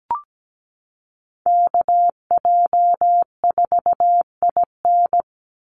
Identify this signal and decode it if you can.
Morse code, heard as a keyed tone, about 17 wpm, 700 Hz. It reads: KJ4IN
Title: USA Call Signs 17wpm